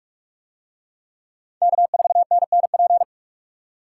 Morse code, heard as a keyed tone, about 40 wpm, 700 Hz. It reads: K4NNP